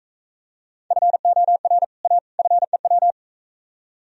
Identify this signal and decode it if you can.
Morse code, heard as a keyed tone, about 44 wpm, 700 Hz. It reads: FOR A FEW